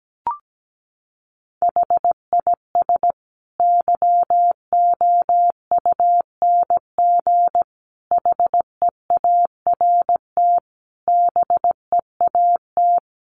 Morse code, heard as a keyed tone, about 17 wpm, 700 Hz. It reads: HIS YOUNG HEART BEAT
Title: Sentences from Top 1300 Words 17wpm